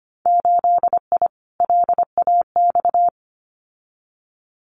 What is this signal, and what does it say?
Morse code, heard as a keyed tone, about 25 wpm, 700 Hz. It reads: 8I LAX
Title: Winter Field Day Exchanges Encore 25wpm